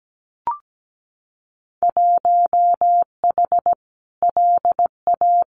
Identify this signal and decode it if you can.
Morse code, heard as a keyed tone, about 17 wpm, 700 Hz. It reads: 1H LA
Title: Winter Field Day Exchanges 17wpm